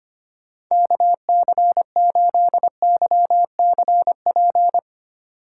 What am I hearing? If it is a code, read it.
Morse code, heard as a keyed tone, about 25 wpm, 700 Hz. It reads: KC8YCP
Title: USA Call Signs - 8x Character Spacing - 25wpm